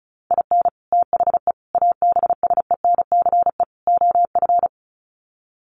Morse code, heard as a keyed tone, about 34 wpm, 700 Hz. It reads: IN THE ABSENCE OF
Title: Sets of 4 Words 35wpm